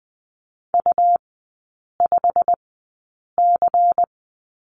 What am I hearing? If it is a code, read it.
Morse code, heard as a keyed tone, about 20 wpm, spaced as for 8 wpm, 700 Hz. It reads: U5C
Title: 100 Lesson 17e - Letter u - Three Characters 20wpm